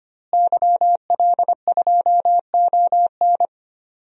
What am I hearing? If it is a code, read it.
Morse code, heard as a keyed tone, about 25 wpm, 700 Hz. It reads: YL2ON
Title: DXCC-100 Call Signs - 5 Characters - Repeated 2x - 25wpm